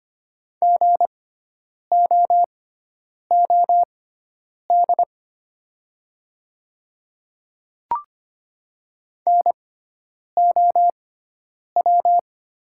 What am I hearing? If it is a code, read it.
Morse code, heard as a keyed tone, about 25 wpm, spaced as for 9 wpm, 700 Hz. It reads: GOOD NOW
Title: 6x Character Spacing - Top 500 Words - 25wpm